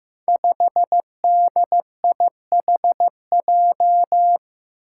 Morse code, heard as a keyed tone, about 15 wpm, 700 Hz. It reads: 5DIHJ
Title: Five Challenging Letter-Number Combinations 15wpm